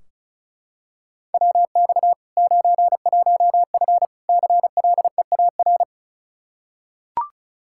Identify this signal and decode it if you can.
Morse code, heard as a keyed tone, about 34 wpm, 700 Hz. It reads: WX 91F CLEAR
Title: K1USN SST QSO Practice 35wpm